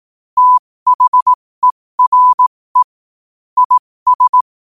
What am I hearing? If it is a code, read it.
Morse code, heard as a keyed tone, about 18 wpm, spaced as for 15 wpm, 1000 Hz. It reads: THERE IS
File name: qotdfiles / 15wpm